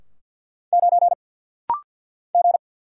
Morse code, heard as a keyed tone, about 47 wpm, 690 Hz. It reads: 9 G
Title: Single Letter-Number Alt - Rapid-Fire 50wpm